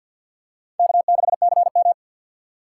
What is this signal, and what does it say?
Morse code, heard as a keyed tone, about 50 wpm, 700 Hz. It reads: K6CM
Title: USA Call Signs - 4 Characters - 6x Character Spacing - 50wpm